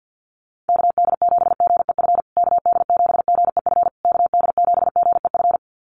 Morse code, heard as a keyed tone, about 50 wpm, 700 Hz. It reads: KD7GEF KD7GEF KD7GEF
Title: USA Call Signs Repeated 3x 50wpm